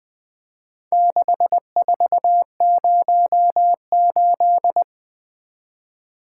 Morse code, heard as a keyed tone, about 23 wpm, 700 Hz. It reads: 6408
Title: Speed-Racing - Four Letter-Number Combinations - 20wpm